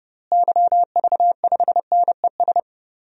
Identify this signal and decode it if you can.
Morse code, heard as a keyed tone, about 32 wpm, 700 Hz. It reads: YV5NES
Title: DXCC-100 Call Signs - 6 Characters - Repeated 2x - 30wpm